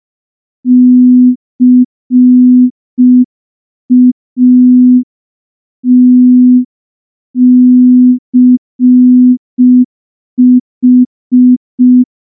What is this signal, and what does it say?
Morse code, heard as a keyed tone, about 5 wpm, 250 Hz. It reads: CATCH
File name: morse / morse-training-05WPM